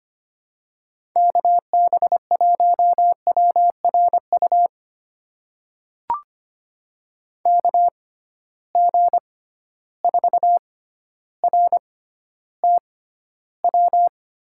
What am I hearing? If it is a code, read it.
Morse code, heard as a keyed tone, about 25 wpm, spaced as for 13 wpm, 700 Hz. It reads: KB1WRU KG4RTW
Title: USA Call Signs - 6 Characters - 6x Character Spacing - 25wpm